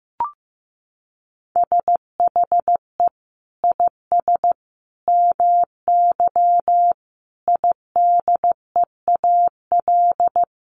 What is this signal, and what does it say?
Morse code, heard as a keyed tone, about 15 wpm, 700 Hz. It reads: SHE IS MY IDEAL
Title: Sentences from Top 1400 Words 15wpm